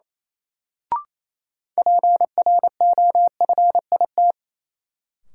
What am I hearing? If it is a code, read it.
Morse code, heard as a keyed tone, about 28 wpm, 700 Hz. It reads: PROFIT
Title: Sentences from Top 800 Words - New Word Drill - 28wpm